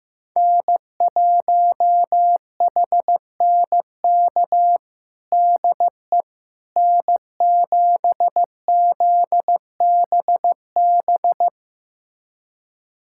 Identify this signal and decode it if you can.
Morse code, heard as a keyed tone, about 16 wpm, 700 Hz. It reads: N1HNK DE N7ZBB
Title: USA Callsign DE Callsign - 5 Characters - 15wpm